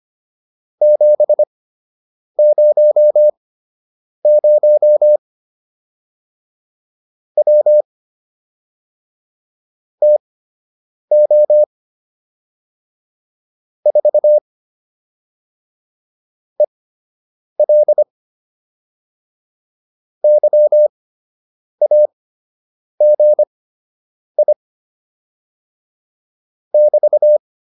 Morse code, heard as a keyed tone, about 25 wpm, spaced as for 8 wpm, 600 Hz. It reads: 700 W TO 4 EL YAGI <BT>